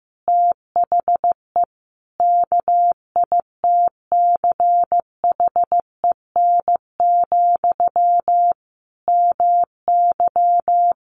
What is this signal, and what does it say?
Morse code, heard as a keyed tone, about 15 wpm, 700 Hz. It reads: THE KITCHEN, MY
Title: Sentences from Top 1400 Words 15wpm